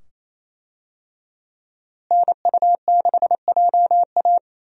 Morse code, heard as a keyed tone, about 28 wpm, 700 Hz. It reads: NU6JA
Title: USA Call Signs - 5 Characters - Repeated 2x - 28wpm